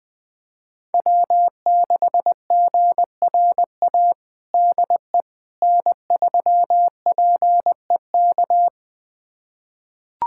Morse code, heard as a keyed tone, about 21 wpm, 700 Hz. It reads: W6GRA DE N3PEK
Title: USA Callsign DE Callsign 20wpm